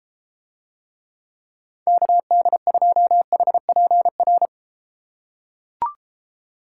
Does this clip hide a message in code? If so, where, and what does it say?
Yes: Morse code from 1.9 to 4.5 s, heard as a keyed tone, about 33 wpm, 700 Hz. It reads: KD2HPR